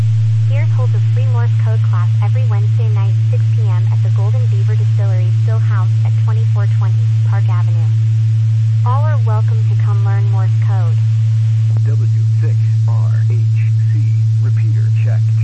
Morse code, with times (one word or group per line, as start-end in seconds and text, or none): none